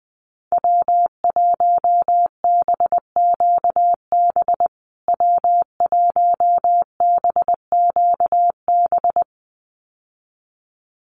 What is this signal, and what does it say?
Morse code, heard as a keyed tone, about 20 wpm, 700 Hz. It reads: W1BQB W1BQB
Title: USA Call Signs Repeated 2x 20wpm